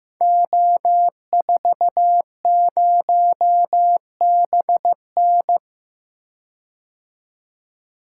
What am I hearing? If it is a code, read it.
Morse code, heard as a keyed tone, about 15 wpm, 700 Hz. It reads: O40BN
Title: Code Groups - 5 Letter:Numbers - 15wpm